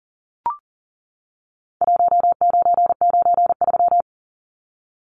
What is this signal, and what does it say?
Morse code, heard as a keyed tone, about 40 wpm, 700 Hz. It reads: 1993
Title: Numbers - Contest Style - 4-Digit 40wpm